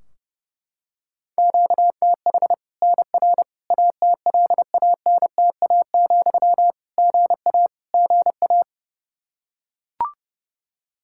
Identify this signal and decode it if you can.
Morse code, heard as a keyed tone, about 30 wpm, 700 Hz. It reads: QTH NR ATLANTA, GA GA
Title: Practice QSO - Exchange 1 - Beginning - 30wpm